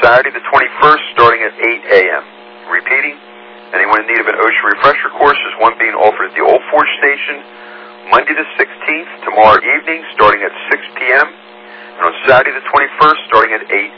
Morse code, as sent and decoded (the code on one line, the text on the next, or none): none